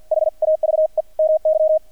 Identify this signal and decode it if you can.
Morse code, heard as a keyed tone, about 48 wpm, spaced as for 42 wpm, 630 Hz. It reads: HAVE MY
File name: archived / CRSnet